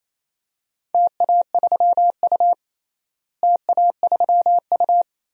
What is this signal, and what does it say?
Morse code, heard as a keyed tone, about 27 wpm, 700 Hz. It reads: TA3U TA3U
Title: DXCC-100 Call Signs - 4 Characters - Repeated 2x - 28wpm